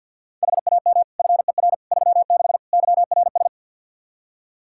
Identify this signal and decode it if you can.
Morse code, heard as a keyed tone, about 52 wpm, 700 Hz. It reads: HAM FER 36 YRS
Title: Practice QSO - Exchange 3 - More Info - 50wpm